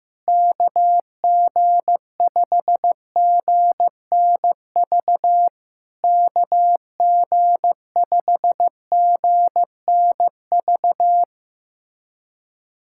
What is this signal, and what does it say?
Morse code, heard as a keyed tone, about 15 wpm, 700 Hz. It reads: KG5GNV KG5GNV
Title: SOTA QSO Practice 15wpm